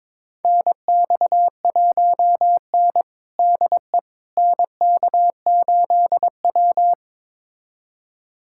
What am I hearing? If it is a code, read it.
Morse code, heard as a keyed tone, about 22 wpm, 700 Hz. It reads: NX1N DE NK8W
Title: USA Callsign DE Callsign - 4 Characters - 22wpm